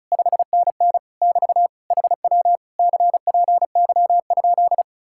Morse code, heard as a keyed tone, about 46 wpm, 700 Hz. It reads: 5NN <BT> HW CPY?